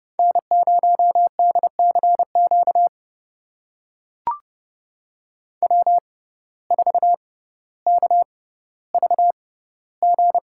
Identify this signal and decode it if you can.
Morse code, heard as a keyed tone, about 31 wpm, spaced as for 16 wpm, 700 Hz. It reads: N0DCQ W4KVG
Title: USA Call Signs - 5 Characters - 6x Character Spacing - 30wpm